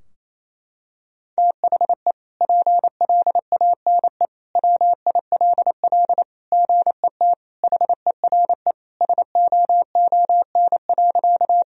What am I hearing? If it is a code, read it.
Morse code, heard as a keyed tone, about 28 wpm, 700 Hz. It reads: THE PLANE WILL GET HERE SOON.
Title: Sentences from Top 1000 Words - Review Part 3 - 28wpm